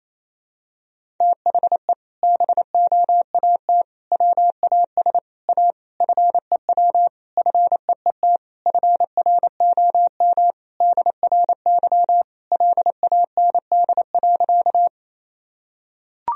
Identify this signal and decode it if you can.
Morse code, heard as a keyed tone, about 28 wpm, 700 Hz. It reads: THE BOAT WAS A FEW FEET FROM DRY LAND.
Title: Sentences from Top 1000 Words - Review Part 3 - 28wpm